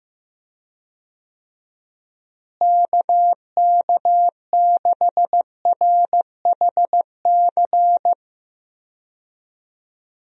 Morse code, heard as a keyed tone, about 15 wpm, 700 Hz. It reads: KK6RHC
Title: USA Call Signs - 6 Characters - Repeated 3x - 15wpm